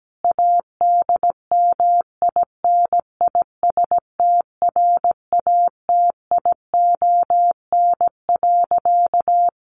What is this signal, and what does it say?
Morse code, heard as a keyed tone, about 17 wpm, 700 Hz. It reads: ADMINISTRATION.
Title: Sentences from Top 1300 Words 17wpm